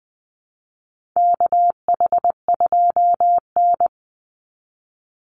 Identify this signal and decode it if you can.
Morse code, heard as a keyed tone, about 20 wpm, 700 Hz. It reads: KH2N
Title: Call Signs Repeated 3x - 20wpm